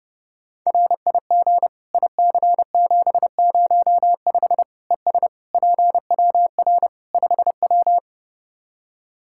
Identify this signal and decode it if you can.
Morse code, heard as a keyed tone, about 30 wpm, 700 Hz. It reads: RIG IC705 ES PWR 5W
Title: Practice QSO - Exchange 2 - Info - 30wpm